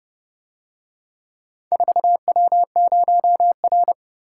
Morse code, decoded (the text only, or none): 4W0R